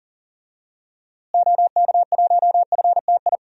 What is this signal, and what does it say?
Morse code, heard as a keyed tone, about 37 wpm, 700 Hz. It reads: OK1FTI
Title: DXCC-100 Call Signs - 6 Characters - Repeated 2x - 40wpm